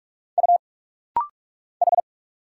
Morse code, heard as a keyed tone, about 46 wpm, 700 Hz. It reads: U H
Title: Single Letters - Rapid-Fire 45wpm